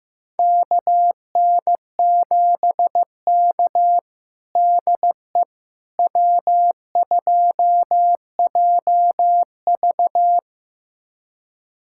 Morse code, heard as a keyed tone, about 16 wpm, 700 Hz. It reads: KN7K DE W2JV
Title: USA Callsign DE Callsign - 4 Characters - 15wpm